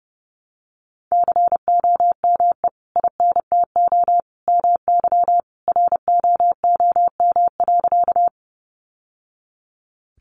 Morse code, heard as a keyed tone, about 30 wpm, 700 Hz. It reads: COME INTO MY ROOM.
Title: Sentences from Top 600 Words 30wpm